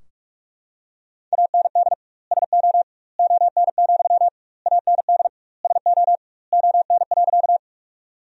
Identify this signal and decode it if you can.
Morse code, heard as a keyed tone, about 46 wpm, 700 Hz. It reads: AND SO ON, AND SO ON.
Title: 2x Word-Spacing - Sentences from Top 100 Words - 45wpm